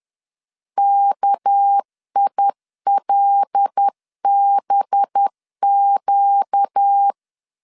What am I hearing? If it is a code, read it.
Morse code, heard as a keyed tone, about 10 wpm, 800 Hz. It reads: KILBQ
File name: CW / DET / 1 6 1500C S C 10wpm 4252